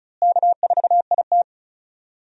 Morse code, heard as a keyed tone, about 36 wpm, 700 Hz. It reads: K4IT
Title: USA Call Signs - 4 Characters - Repeated 3x - 35wpm